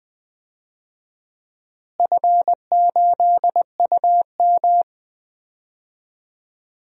Morse code, heard as a keyed tone, about 20 wpm, 700 Hz. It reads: F8UM
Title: DXCC-100 Call Signs - 4 Characters - Repeated 2x - 20wpm